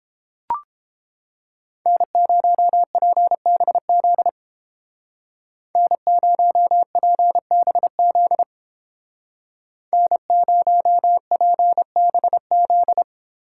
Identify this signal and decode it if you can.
Morse code, heard as a keyed tone, about 30 wpm, 700 Hz. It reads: N0PBZ N0PBZ N0PBZ